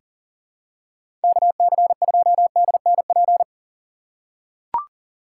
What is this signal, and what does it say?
Morse code, heard as a keyed tone, about 40 wpm, 700 Hz. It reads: KC2DNP
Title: USA Call Signs - 6 Characters - 6x Character Spacing - 40wpm